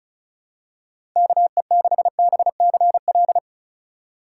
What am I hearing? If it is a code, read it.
Morse code, heard as a keyed tone, about 34 wpm, 700 Hz. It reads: KE6BCL
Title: USA Call Signs - 6 Characters - 2x Character Spacing - 35wpm